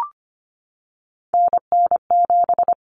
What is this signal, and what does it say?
Morse code, heard as a keyed tone, about 25 wpm, 700 Hz. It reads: NN7
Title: Common Cut Numbers - 3-Digit 25wpm